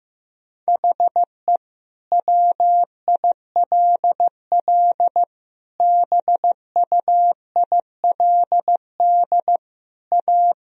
Morse code, heard as a keyed tone, about 15 wpm, 700 Hz. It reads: HE WILL BUILD A
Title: Sentences from Top 1000 Words - Review Part 1 - 15wpm